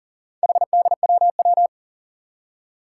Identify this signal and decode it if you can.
Morse code, heard as a keyed tone, about 40 wpm, 700 Hz. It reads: HDWW
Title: Four Challenging Letter-Number Combinations 40wpm